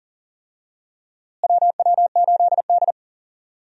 Morse code, heard as a keyed tone, about 40 wpm, 700 Hz. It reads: WW8D
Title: USA Call Signs - 4 Characters - Repeated 2x - 40wpm